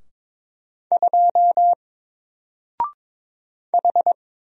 Morse code, heard as a keyed tone, about 22 wpm, 700 Hz. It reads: 2 H
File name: Single-Letter-Number-22wpm